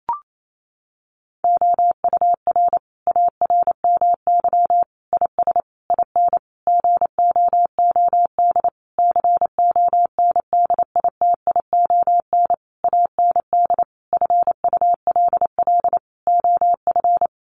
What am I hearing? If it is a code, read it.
Morse code, heard as a keyed tone, about 28 wpm, 700 Hz. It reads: OUR ARMY IS IN GOOD CONDITION AND FULL OF
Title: Sentences from Top 1500 Words 28wpm